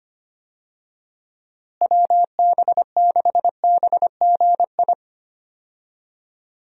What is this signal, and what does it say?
Morse code, heard as a keyed tone, about 25 wpm, 700 Hz. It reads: WB6BGI
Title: USA Call Signs - 6 Characters - Repeated 2x - 25wpm